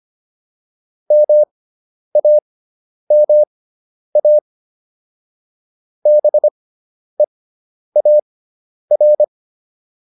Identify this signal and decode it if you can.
Morse code, heard as a keyed tone, about 25 wpm, spaced as for 10 wpm, 600 Hz. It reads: MAMA BEAR